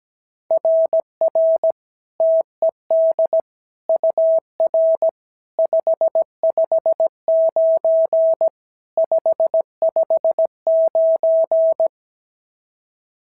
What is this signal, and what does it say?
Morse code, heard as a keyed tone, about 16 wpm, 650 Hz. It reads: RR TED UR 559 559
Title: SOTA QSO Practice 17wpm